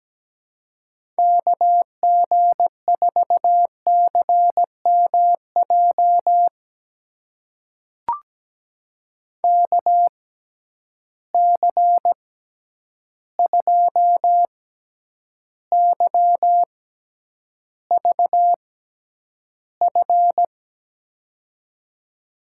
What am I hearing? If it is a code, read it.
Morse code, heard as a keyed tone, about 17 wpm, spaced as for 9 wpm, 700 Hz. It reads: KG4CMJ KC2YVF
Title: USA Call Signs - 6 Characters - 6x Character Spacing - 17wpm